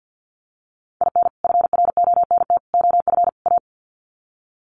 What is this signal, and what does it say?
Morse code, heard as a keyed tone, about 50 wpm, 700 Hz. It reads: IN FRONT OF A
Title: Sets of 4 Words 50wpm